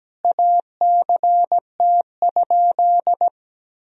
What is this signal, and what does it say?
Morse code, heard as a keyed tone, about 17 wpm, 700 Hz. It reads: ACT?
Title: Sentences from Top 1000 Words - Review Part 2 - 17wpm